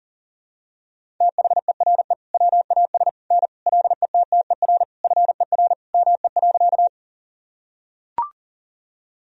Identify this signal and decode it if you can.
Morse code, heard as a keyed tone, about 40 wpm, 700 Hz. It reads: THERE WAS N LETTER FER ME.
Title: Sentences from Top 300 Words and Abbreviations 40wpm